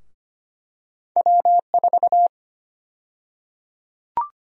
Morse code, heard as a keyed tone, about 25 wpm, 700 Hz. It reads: W4
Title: USA Call Sign Prefixes - 25wpm